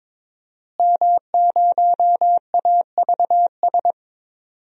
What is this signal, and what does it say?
Morse code, heard as a keyed tone, about 22 wpm, 700 Hz. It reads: M0AVS
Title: DXCC-100 Call Signs - 5 Characters - Repeated 3x - 22wpm